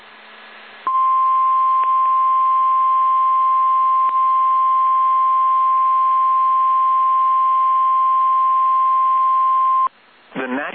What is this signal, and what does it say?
Emergency Alert System (SAME) tones (the 1050 Hz attention tone), with speech. It is heard from 0.9 to 9.9 s.